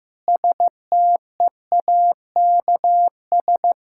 Morse code, heard as a keyed tone, about 15 wpm, 700 Hz. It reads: STEAKS
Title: Pangram Challenge v2 - 15wpm